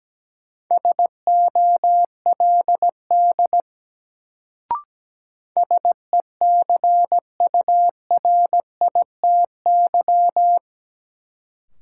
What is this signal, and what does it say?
Morse code, heard as a keyed tone, about 17 wpm, 700 Hz. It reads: SOLD SECURITY
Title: Sentences from Top 1400 Words - New Word Drill Alt - 17wpm